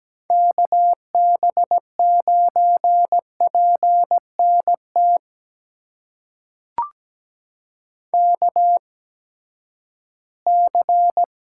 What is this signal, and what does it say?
Morse code, heard as a keyed tone, about 17 wpm, spaced as for 12 wpm, 700 Hz. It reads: KB9PNT KC
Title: USA Call Signs - 6 Characters - 8x Character Spacing - 17wpm